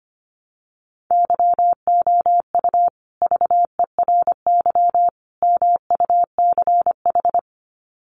Morse code, heard as a keyed tone, about 25 wpm, 700 Hz. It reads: YOU VERY MUCH
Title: Sets of 3 Words 25wpm